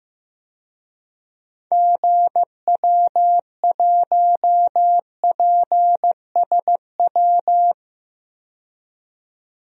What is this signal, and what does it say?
Morse code, heard as a keyed tone, about 15 wpm, 700 Hz. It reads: GW1PSW
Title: DXCC-100 Call Signs - 6 Characters - 15wpm